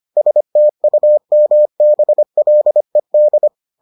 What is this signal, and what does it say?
Morse code, heard as a keyed tone, about 25 wpm, 600 Hz. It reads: STUMBLED